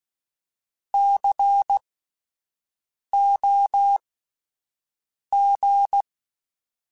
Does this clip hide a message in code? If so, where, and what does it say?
Yes: Morse code from 0.9 to 6.0 s, heard as a keyed tone, about 16 wpm, spaced as for 5 wpm, 800 Hz. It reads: COG